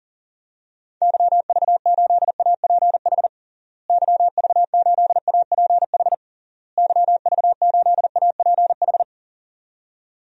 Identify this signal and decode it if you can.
Morse code, heard as a keyed tone, about 40 wpm, 700 Hz. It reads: YV8APH YV8APH YV8APH
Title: DXCC-100 Call Signs - 6 Characters - Repeated 3x - 40wpm